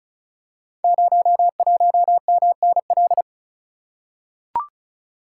Morse code, heard as a keyed tone, about 36 wpm, 700 Hz. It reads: 01MNL
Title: Code Groups - 5 Letter:Numbers - 35wpm